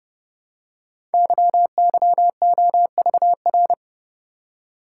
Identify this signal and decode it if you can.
Morse code, heard as a keyed tone, about 30 wpm, 700 Hz. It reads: YYOVR
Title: Code Groups - 5 Letters - 30wpm